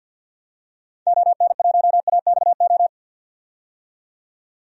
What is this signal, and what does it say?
Morse code, heard as a keyed tone, about 46 wpm, 700 Hz. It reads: ON1AXO
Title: DXCC-100 Call Signs - 6 Characters - 50wpm